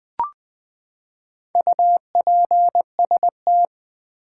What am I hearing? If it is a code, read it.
Morse code, heard as a keyed tone, about 20 wpm, 700 Hz. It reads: UPST
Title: Build A Word - 4 Letters At A Time - 8-Letter Words - 20wpm